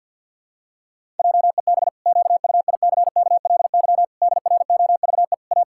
Morse code, heard as a keyed tone, about 52 wpm, 700 Hz. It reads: JEB QUICKLY DROVE A
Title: Pangram Challenge v2 - 50wpm